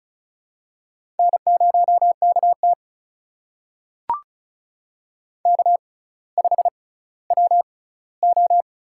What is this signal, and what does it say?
Morse code, heard as a keyed tone, about 36 wpm, spaced as for 18 wpm, 700 Hz. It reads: N0KT K5WO
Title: USA Call Signs - 4 Characters - 6x Character Spacing - 35wpm